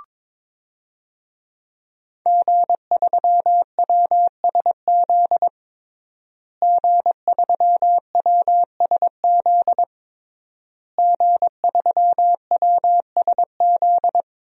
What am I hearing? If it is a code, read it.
Morse code, heard as a keyed tone, about 21 wpm, 700 Hz. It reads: G3WSZ G3WSZ G3WSZ
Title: DXCC-100 Call Signs - 5 Characters - Repeated 3x - 22wpm